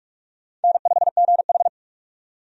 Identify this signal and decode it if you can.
Morse code, heard as a keyed tone, about 46 wpm, 700 Hz. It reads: N5GH
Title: USA Call Signs - 4 Characters - 45wpm